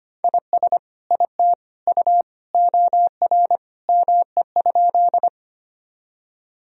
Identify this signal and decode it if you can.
Morse code, heard as a keyed tone, about 25 wpm, 700 Hz. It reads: IS IT U OR ME?
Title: Sentences from Top 200 Words and Abbreviations 25wpm